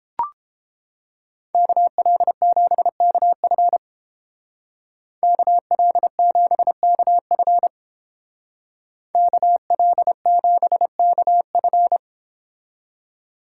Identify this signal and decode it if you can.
Morse code, heard as a keyed tone, about 29 wpm, 700 Hz. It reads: KL7KF KL7KF KL7KF